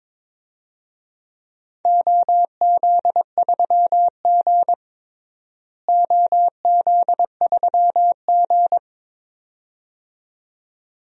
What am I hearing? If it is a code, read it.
Morse code, heard as a keyed tone, about 22 wpm, 700 Hz. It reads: OZ3G OZ3G
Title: DXCC-100 Call Signs - 4 Characters - Repeated 2x - 22wpm